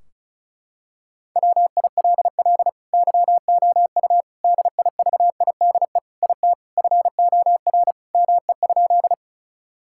Morse code, heard as a keyed tone, about 36 wpm, 700 Hz. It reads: WILL YOU DIVIDE IT FOR ME?
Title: Sentences from Top 1600 Words 35wpm